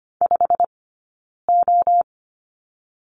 Morse code, heard as a keyed tone, about 25 wpm, 700 Hz. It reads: 5 O
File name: Lesson 15d - Number 5 - Two Characters 25wpm